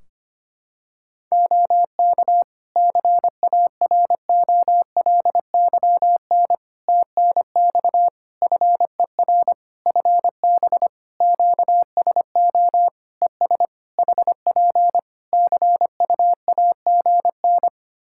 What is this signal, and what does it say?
Morse code, heard as a keyed tone, about 25 wpm, 700 Hz. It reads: OK CAROLYN TNX FER FB QSO ES HP CUAGN
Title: Practice QSO - Exchange 4 - Ending - 25wpm